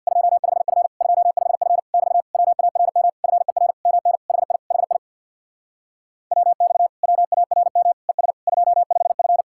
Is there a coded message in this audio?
Yes: Morse code from 0.1 to 9.4 s, heard as a keyed tone, about 65 wpm, 700 Hz. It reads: 25F 25F <BT> WARM FER MN HI HI WX WARM ES 25F